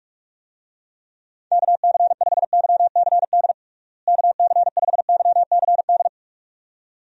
Morse code, heard as a keyed tone, about 46 wpm, 700 Hz. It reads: KC5YCD KC5YCD
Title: USA Call Signs - 6 Characters - Repeated 2x - 45wpm